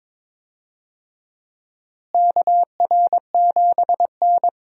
Morse code, heard as a keyed tone, about 22 wpm, 700 Hz. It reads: KR7N
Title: USA Call Signs - 4 Characters - Repeated 2x - 22wpm